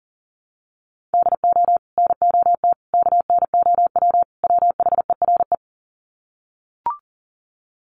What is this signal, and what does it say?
Morse code, heard as a keyed tone, about 37 wpm, 700 Hz. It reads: DO NOT KNOW WHERE
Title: Sets of 4 Words 40wpm